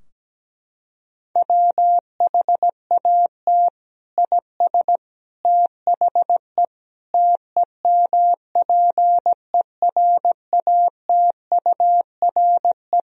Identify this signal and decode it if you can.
Morse code, heard as a keyed tone, about 17 wpm, 700 Hz. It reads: WHAT IS THE TEMPERATURE
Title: Sentences from Top 1000 Words - Review Part 4 - 17wpm